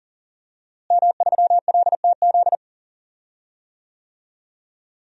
Morse code, heard as a keyed tone, about 40 wpm, 700 Hz. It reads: M3LTZ
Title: DXCC-100 Call Signs - 5 Characters - Repeated 3x - 40wpm